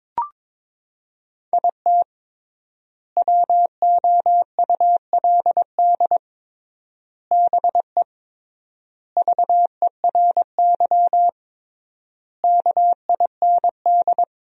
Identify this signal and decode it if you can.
Morse code, heard as a keyed tone, about 22 wpm, 700 Hz. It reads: IT WOULD BE VERY KIND
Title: 3x Word-Spacing - Sentences from Top 200 Words - 22wpm